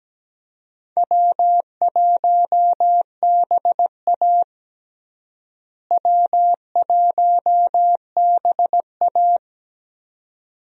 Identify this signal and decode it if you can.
Morse code, heard as a keyed tone, about 17 wpm, 700 Hz. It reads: W1BA W1BA
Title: USA Call Signs - 4 Characters - Repeated 2x - 17wpm